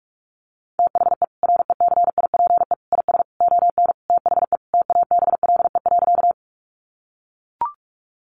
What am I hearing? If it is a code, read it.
Morse code, heard as a keyed tone, about 44 wpm, 700 Hz. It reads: THE RECIPE IS ON THE TABLE.
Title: Sentences from Top 1200 Words 45wpm